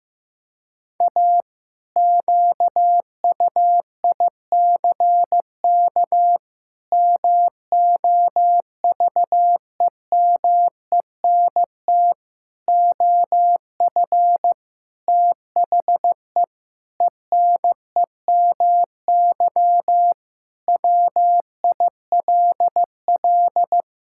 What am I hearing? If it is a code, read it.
Morse code, heard as a keyed tone, about 15 wpm, 700 Hz. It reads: A QUICK MOVEMENT OF THE ENEMY WILL
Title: Pangram Challenge v2 - 15wpm